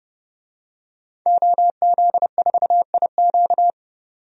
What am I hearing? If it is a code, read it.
Morse code, heard as a keyed tone, about 29 wpm, 700 Hz. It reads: OZ4IQ
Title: DXCC-100 Call Signs - 5 Characters - 30wpm